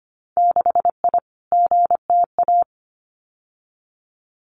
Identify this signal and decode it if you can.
Morse code, heard as a keyed tone, about 25 wpm, 700 Hz. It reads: 6I GTA
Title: Winter Field Day Exchanges Encore 25wpm